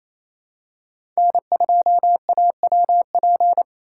Morse code, heard as a keyed tone, about 27 wpm, 700 Hz. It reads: N2AWP
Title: USA Call Signs - 6x Character Spacing - 28wpm